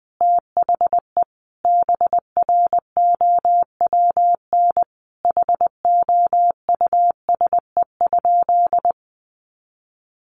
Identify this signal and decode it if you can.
Morse code, heard as a keyed tone, about 20 wpm, 700 Hz. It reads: THE BROWN HOUSE?
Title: Sentences from Top 1200 Words 20wpm